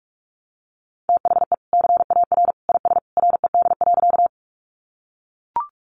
Morse code, heard as a keyed tone, about 44 wpm, 700 Hz. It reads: THE CAR IS RED.
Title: Easy Phrases 45wpm